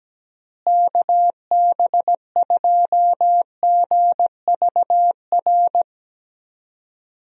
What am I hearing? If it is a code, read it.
Morse code, heard as a keyed tone, about 17 wpm, 700 Hz. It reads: KB2GVR
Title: USA Call Signs - 6 Characters - Repeated 3x - 17wpm